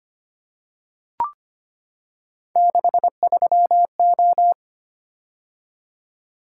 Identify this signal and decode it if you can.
Morse code, heard as a keyed tone, about 25 wpm, 700 Hz. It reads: 63O
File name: Three Letter-Number Combinations - 25wpm